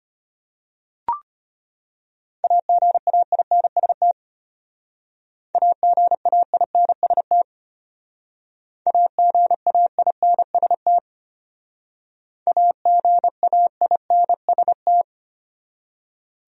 Morse code, heard as a keyed tone, about 32 wpm, 700 Hz. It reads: AGAINST AGAINST AGAINST AGAINST